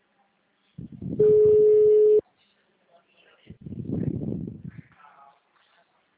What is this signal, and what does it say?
Ringback tone (Europe, 425 Hz), with speech.
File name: out- -2017- - - .56293